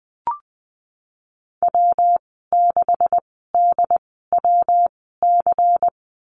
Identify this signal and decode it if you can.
Morse code, heard as a keyed tone, about 20 wpm, spaced as for 14 wpm, 700 Hz. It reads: W6DWC
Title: 147 Lesson 24g - Letter y - Call Signs 20wpm